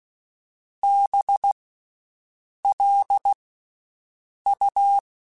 Morse code, heard as a keyed tone, about 16 wpm, 800 Hz. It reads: B L U
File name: Special INGs-mp3s / Special INGs-mp3s / Slow ing @15